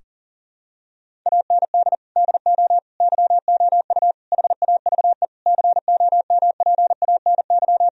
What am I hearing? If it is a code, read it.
Morse code, heard as a keyed tone, about 40 wpm, 700 Hz. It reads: AND DO YOU HAVE COMPANY
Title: Sentences from Top 1000 Words - Review Part 3 - 40wpm